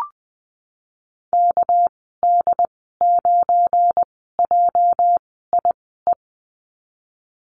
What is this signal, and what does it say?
Morse code, heard as a keyed tone, about 20 wpm, spaced as for 14 wpm, 700 Hz. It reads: KD9JIE
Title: 245 Lesson 39g - Letter z - Call Signs 20wpm